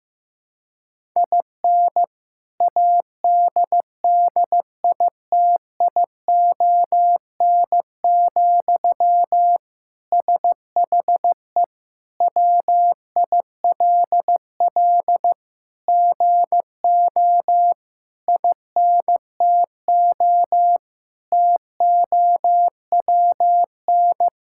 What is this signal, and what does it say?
Morse code, heard as a keyed tone, about 15 wpm, 700 Hz. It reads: IN ADDITION, SHE WILL GO INTO TOWN
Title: Sentences from Top 1000 Words - Review Part 6 - 15wpm